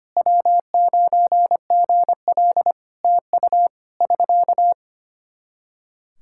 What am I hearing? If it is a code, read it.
Morse code, heard as a keyed tone, about 25 wpm, 700 Hz. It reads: W9GL TU <SK>
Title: Practice QSO - Exchange 4 - Ending - 25wpm